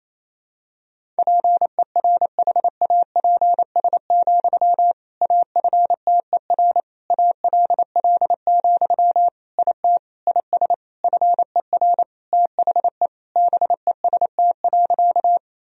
Morse code, heard as a keyed tone, about 27 wpm, 700 Hz. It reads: PERHAPS, AFTER ALL, IT IS FER THE BEST.
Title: Sentences from Top 500 Words and Abbreviations 28wpm